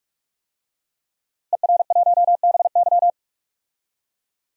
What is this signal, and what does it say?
Morse code, heard as a keyed tone, about 45 wpm, 700 Hz. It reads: ER1BY
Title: DXCC-100 Call Signs - 5 Characters - Repeated 2x - 45wpm